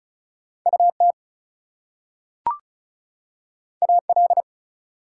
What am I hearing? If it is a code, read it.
Morse code, heard as a keyed tone, about 36 wpm, 700 Hz. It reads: UT AL
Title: US States and Canadian Provinces 35wpm